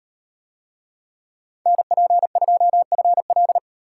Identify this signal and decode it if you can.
Morse code, heard as a keyed tone, about 38 wpm, 700 Hz. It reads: NP2FL